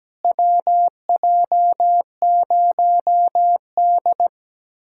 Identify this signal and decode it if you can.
Morse code, heard as a keyed tone, about 17 wpm, 700 Hz. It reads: WJ0D
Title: USA Callsign DE Callsign - 4 Characters - 17wpm